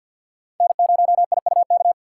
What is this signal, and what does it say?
Morse code, heard as a keyed tone, about 50 wpm, 700 Hz. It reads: N0IUK
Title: USA Call Signs - 5 Characters - 6x Character Spacing - 50wpm